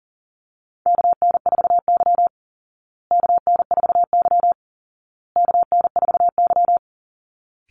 Morse code, heard as a keyed tone, about 40 wpm, 700 Hz. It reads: KN4Y KN4Y KN4Y
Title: Call Signs Repeated 2x - 40wpm